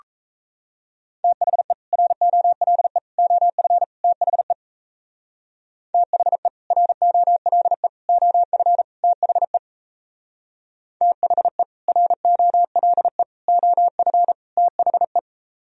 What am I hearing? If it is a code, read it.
Morse code, heard as a keyed tone, about 34 wpm, 690 Hz. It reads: THE ROLE OF THE THE ROLE OF THE THE ROLE OF THE
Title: Speed-Racing - Sets of 4 Words 28wpm